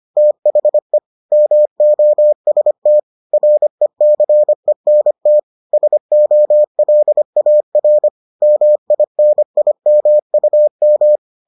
Morse code, heard as a keyed tone, about 25 wpm, 600 Hz. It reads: THE MOST RECENT SOLAR MINIMUM